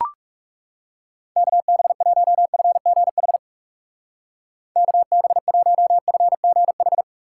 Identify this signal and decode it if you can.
Morse code, heard as a keyed tone, about 38 wpm, 700 Hz. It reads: KB1FGH KB1FGH